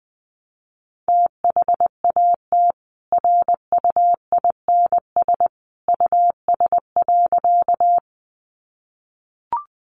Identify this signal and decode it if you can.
Morse code, heard as a keyed tone, about 20 wpm, 700 Hz. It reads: THAT RUINS US.
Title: A Sound Of Thunder 20wpm